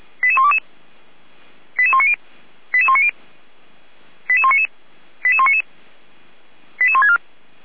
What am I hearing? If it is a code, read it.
Five-tone selective call (ZVEI), heard as quick runs of five tones, tones of about 80 ms. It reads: 88122 88190 88190 88199 88199 88164